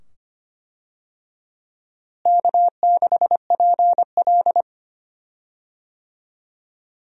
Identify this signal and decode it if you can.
Morse code, heard as a keyed tone, about 25 wpm, 700 Hz. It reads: K6PL
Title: USA Call Signs - 4 Characters - Repeated 2x - 25wpm